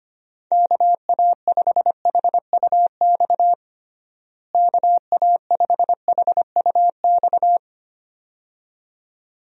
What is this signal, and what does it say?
Morse code, heard as a keyed tone, about 25 wpm, 700 Hz. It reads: KA5HUX KA5HUX
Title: USA Call Signs - 6 Characters - Repeated 2x - 25wpm